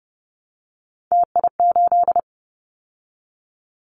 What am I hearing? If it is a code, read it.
Morse code, heard as a keyed tone, about 30 wpm, 700 Hz. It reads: TI8
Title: Callsign Prefixes for Top 100 DXCC - 30wpm